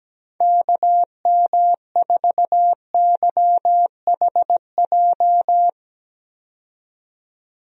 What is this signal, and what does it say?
Morse code, heard as a keyed tone, about 17 wpm, 700 Hz. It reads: KM4YHJ
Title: USA Call Signs - 6 Characters - 17wpm